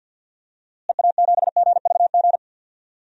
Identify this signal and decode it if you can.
Morse code, heard as a keyed tone, about 51 wpm, 700 Hz. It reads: EA7GVG
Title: DXCC-100 Call Signs - 6 Characters - Repeated 3x - 50wpm